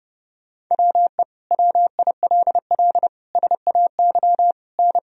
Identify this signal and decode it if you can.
Morse code, heard as a keyed tone, about 30 wpm, 700 Hz. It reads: WE WILL SAY N
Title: Sentences from Top 200 Words and Abbreviations 30wpm